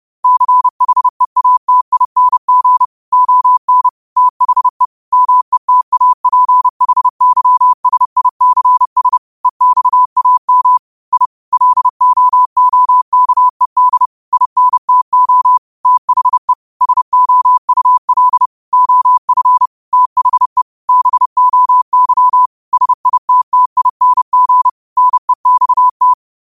Morse code, heard as a keyed tone, about 30 wpm, 1000 Hz. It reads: CHEATING ON THE METAPHYSICS EXAM I LOOKED INTO THE SOUL OF THE BOY SITTING NEXT
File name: 30wpm